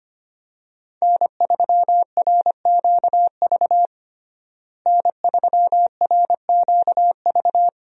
Morse code, heard as a keyed tone, about 25 wpm, 700 Hz. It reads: N3RQV N3RQV
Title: USA Call Signs - 5 Characters - Repeated 3x - 25wpm